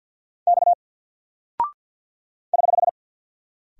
Morse code, heard as a keyed tone, about 48 wpm, 700 Hz. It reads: X <HH>
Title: Single Character (Letter:Number:Punctuation:Prosign) Alt 50wpm